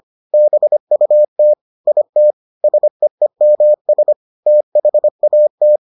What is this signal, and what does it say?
Morse code, heard as a keyed tone, about 25 wpm, 600 Hz. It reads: BUT IT SEEMS THAT